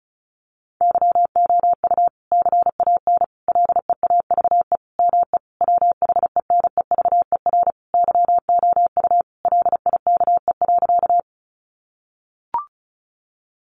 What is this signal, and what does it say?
Morse code, heard as a keyed tone, about 34 wpm, 700 Hz. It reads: YOU CAN LEAVE ME WHENEVER YOU LIKE.
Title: Sentences from Top 1200 Words 35wpm